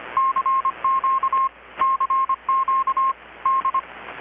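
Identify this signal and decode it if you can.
Morse code, heard as a keyed tone, about 25 wpm, 1070 Hz, with radio noise. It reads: CQ CQ D
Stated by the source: Morse from KPH's Night of Nights XX maritime radiotelegraphy event, received in Chiapas, Mexico about 3,800 km away on a Kenwood R-600 with a grounded 10 m long-wire antenna.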